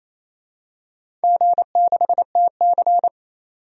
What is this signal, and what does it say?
Morse code, heard as a keyed tone, about 28 wpm, 700 Hz. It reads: G6TC
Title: DXCC-100 Call Signs - 4 Characters - 28wpm